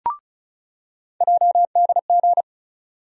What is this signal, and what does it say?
Morse code, heard as a keyed tone, about 36 wpm, 700 Hz. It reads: JDG
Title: USA Call Sign Postfixes - 3 Letters 35wpm